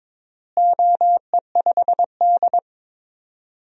Morse code, heard as a keyed tone, about 22 wpm, 700 Hz. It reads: OE5D
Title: DXCC-100 Call Signs - 4 Characters - Repeated 3x - 22wpm